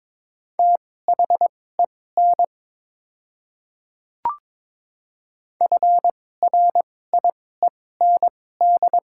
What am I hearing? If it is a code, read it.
Morse code, heard as a keyed tone, about 22 wpm, spaced as for 16 wpm, 700 Hz. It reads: THEN FRIEND
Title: 2x Character Spacing - Top 300 Words - 22wpm